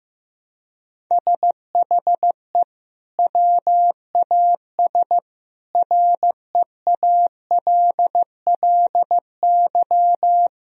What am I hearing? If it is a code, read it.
Morse code, heard as a keyed tone, about 15 wpm, 700 Hz. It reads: SHE WAS REALLY
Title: Sentences from Top 1000 Words - Review Part 4 - 15wpm